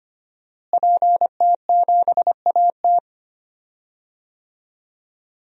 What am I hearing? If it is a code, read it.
Morse code, heard as a keyed tone, about 25 wpm, 700 Hz. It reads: PT7AT
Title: DXCC-100 Call Signs - 5 Characters - Repeated 2x - 25wpm